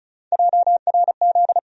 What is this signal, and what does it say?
Morse code, heard as a keyed tone, about 36 wpm, 700 Hz. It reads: JRZ
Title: USA Call Sign Postfixes - 3 Letters 35wpm